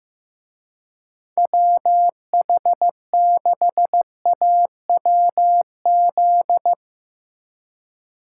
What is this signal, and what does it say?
Morse code, heard as a keyed tone, about 15 wpm, 700 Hz. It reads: WH6AWZ
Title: Speed Racing - USA 2x3 Call Signs 15wpm